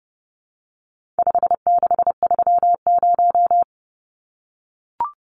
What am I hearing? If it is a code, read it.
Morse code, heard as a keyed tone, about 30 wpm, 700 Hz. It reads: 5630
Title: Numbers - 4-Digit 30wpm